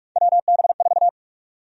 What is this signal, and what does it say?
Morse code, heard as a keyed tone, about 46 wpm, 700 Hz. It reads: WB4
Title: USA Call Sign Prefixes - 45wpm